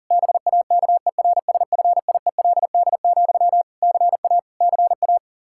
Morse code, heard as a keyed tone, about 40 wpm, 700 Hz. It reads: BAKERSFIELD, CA CA
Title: K1USN SST QSO Practice 40wpm